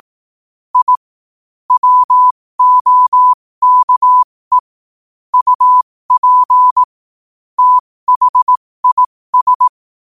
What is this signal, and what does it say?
Morse code, heard as a keyed tone, about 18 wpm, spaced as for 15 wpm, 1000 Hz. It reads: I WOKE UP THIS
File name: qotdfiles / 15wpm